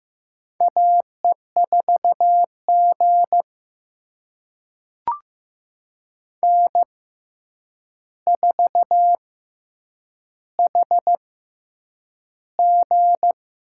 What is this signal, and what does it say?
Morse code, heard as a keyed tone, about 15 wpm, spaced as for 8 wpm, 700 Hz. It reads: AE4G N4HG
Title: USA Call Signs - 4 Characters - 6x Character Spacing - 15wpm